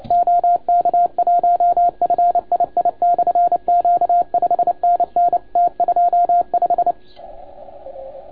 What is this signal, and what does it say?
Morse code, heard as a keyed tone, about 30 wpm, 690 Hz, with radio noise. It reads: OK1FII/Q 5NN T25
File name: Archiv / CUC287 / OK1FGD CUC 287